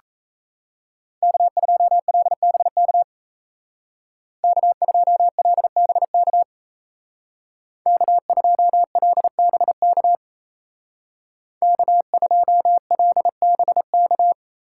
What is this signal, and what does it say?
Morse code, heard as a keyed tone, about 36 wpm, 700 Hz. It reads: K2LBK K2LBK K2LBK K2LBK